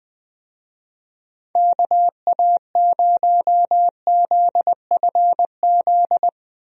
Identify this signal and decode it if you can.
Morse code, heard as a keyed tone, about 20 wpm, 700 Hz. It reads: KA0ZFZ
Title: USA Call Signs - 6 Characters - Repeated 2x - 20wpm